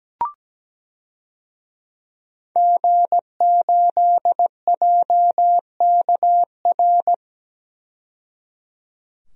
Morse code, heard as a keyed tone, about 17 wpm, 700 Hz. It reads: G8JKR
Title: DXCC-100 Call Signs - 5 Characters - 17wpm